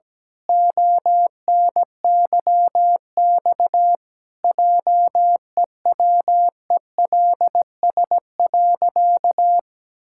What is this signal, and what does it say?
Morse code, heard as a keyed tone, about 17 wpm, 700 Hz. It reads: ONYX JEWELS.
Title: Pangram Challenge v2 - 17wpm